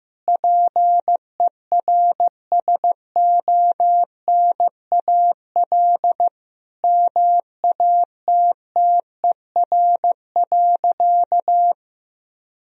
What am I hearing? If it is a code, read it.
Morse code, heard as a keyed tone, about 15 wpm, 700 Hz. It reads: PERSONAL MATTER.
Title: Sentences from Top 1000 Words - Review Part 4 - 15wpm